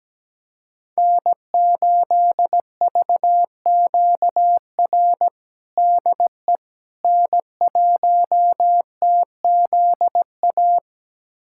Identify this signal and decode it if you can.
Morse code, heard as a keyed tone, about 17 wpm, 700 Hz. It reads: N8VQR DE N1TZA
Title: USA Callsign DE Callsign - 5 Characters - 17wpm